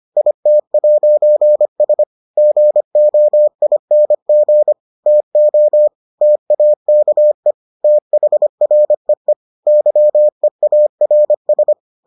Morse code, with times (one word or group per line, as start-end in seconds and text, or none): 0.2-2.0 IT'S
2.4-4.7 GOING
5.1-5.9 TO
6.2-7.5 TAKE
7.8-9.3 THREE
9.7-11.7 YEARS